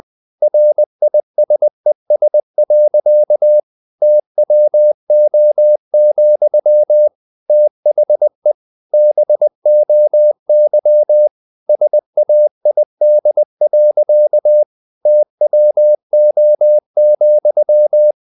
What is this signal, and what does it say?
Morse code, heard as a keyed tone, about 20 wpm, 600 Hz. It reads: RISES. TWO, THE BOY SAID. TWO,
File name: the old man and the sea / the old man and the sea part 1 20wpm